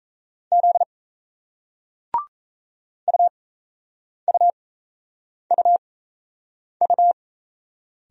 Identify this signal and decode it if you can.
Morse code, heard as a keyed tone, about 38 wpm, spaced as for 8 wpm, 700 Hz. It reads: Z UUUU